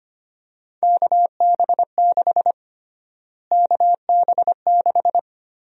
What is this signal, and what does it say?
Morse code, heard as a keyed tone, about 25 wpm, 700 Hz. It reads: KB6 KB6
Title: USA Call Sign Prefixes - Repeated 2x - 25wpm